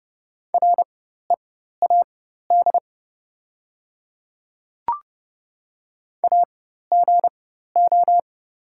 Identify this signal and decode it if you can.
Morse code, heard as a keyed tone, about 30 wpm, spaced as for 14 wpm, 690 Hz. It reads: READ AGO
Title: 4x Character Spacing - Top 500 Words - 30wpm